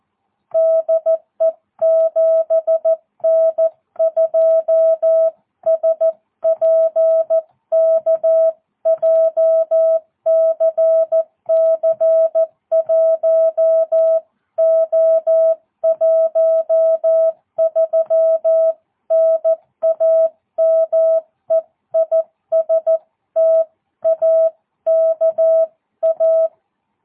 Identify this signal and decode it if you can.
Morse code, heard as a keyed tone, about 14 wpm, 660 Hz. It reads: DE 7N2SPK JCC1O13 NAME IS TAKA
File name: jh8cbh / 2022cw